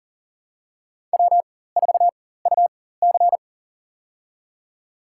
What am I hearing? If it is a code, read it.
Morse code, heard as a keyed tone, about 40 wpm, spaced as for 19 wpm, 700 Hz. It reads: W4UC
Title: USA Call Signs - 4 Characters - 4x Character Spacing - 40wpm